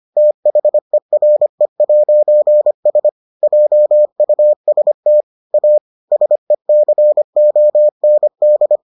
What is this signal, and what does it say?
Morse code, heard as a keyed tone, about 25 wpm, 600 Hz. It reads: THERE'S JUST A SECOND